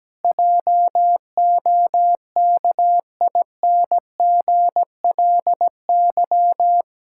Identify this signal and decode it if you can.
Morse code, heard as a keyed tone, about 17 wpm, 700 Hz. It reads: JOKINGLY
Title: Pangram Challenge v2 - 17wpm